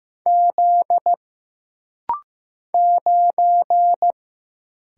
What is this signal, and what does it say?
Morse code, heard as a keyed tone, about 15 wpm, 700 Hz. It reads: Z 9
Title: Single Letter-Number Alt - Rapid-Fire 15wpm